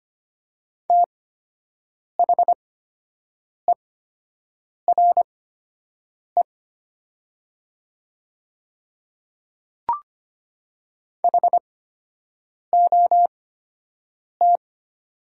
Morse code, heard as a keyed tone, about 25 wpm, spaced as for 7 wpm, 700 Hz. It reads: THERE HOT
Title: 8x Character Spacing - Top 100 Words - 25wpm